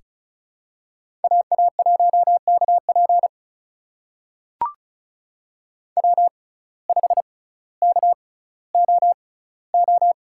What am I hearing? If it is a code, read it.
Morse code, heard as a keyed tone, about 36 wpm, spaced as for 18 wpm, 700 Hz. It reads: AA1KP W5KOO
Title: USA Call Signs - 6x Character Spacing - 35wpm